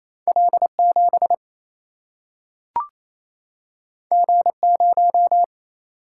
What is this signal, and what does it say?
Morse code, heard as a keyed tone, about 28 wpm, 700 Hz. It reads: L 7 G 0
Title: Single-Letter Single-Number 28wpm